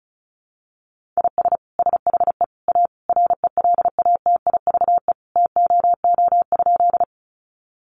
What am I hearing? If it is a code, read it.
Morse code, heard as a keyed tone, about 34 wpm, 700 Hz. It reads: IS SHE A RELATIVE TOO?
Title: Sentences from Top 1400 Words 35wpm